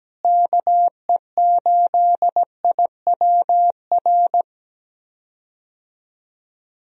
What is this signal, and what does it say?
Morse code, heard as a keyed tone, about 17 wpm, 700 Hz. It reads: KE8IWR
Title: USA Call Signs - 6 Characters - Repeated 2x - 17wpm